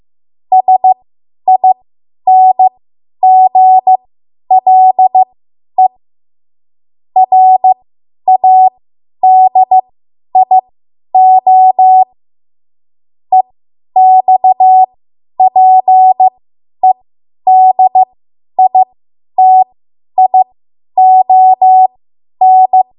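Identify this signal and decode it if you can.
Morse code, heard as a keyed tone, about 15 wpm, spaced as for 10 wpm, 750 Hz. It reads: SINGLE RADIO EXPEDITION
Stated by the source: Morse code practice run at 10 WPM from a code practice archive.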